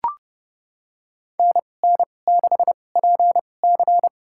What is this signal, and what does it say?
Morse code, heard as a keyed tone, about 29 wpm, spaced as for 21 wpm, 700 Hz. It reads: NN6PC
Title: USA Call Signs - 5 Characters - 2x Character Spacing - 30wpm